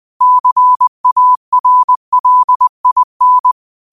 Morse code, heard as a keyed tone, about 20 wpm, 1000 Hz. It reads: CARLIN
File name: qotdfiles / 20wpm